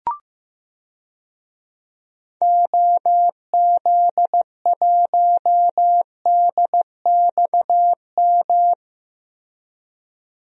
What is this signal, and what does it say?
Morse code, heard as a keyed tone, about 15 wpm, 700 Hz. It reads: OZ1DXM
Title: DXCC-100 Call Signs - 6 Characters - 15wpm